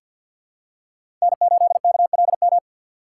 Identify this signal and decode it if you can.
Morse code, heard as a keyed tone, about 47 wpm, 700 Hz. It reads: N8KLM
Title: USA Call Signs - 5 Characters - 3x Character Spacing - 50wpm